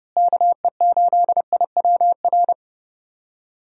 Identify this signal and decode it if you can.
Morse code, heard as a keyed tone, about 30 wpm, 700 Hz. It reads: KE8IWR
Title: USA Call Signs - 6 Characters - Repeated 2x - 30wpm